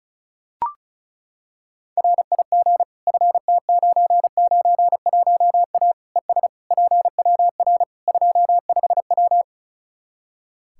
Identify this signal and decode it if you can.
Morse code, heard as a keyed tone, about 34 wpm, 700 Hz. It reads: RIG FT991A ES PWR 25W
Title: Practice QSO - Exchange 2 - Info - 35wpm